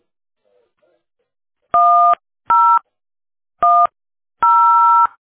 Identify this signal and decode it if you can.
DTMF, heard as telephone touch tones, tones of about 360 ms. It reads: 1010